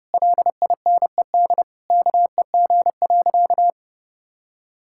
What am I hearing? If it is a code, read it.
Morse code, heard as a keyed tone, about 30 wpm, 700 Hz. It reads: LINED KEG.
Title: Pangram Challenge v2 - 30wpm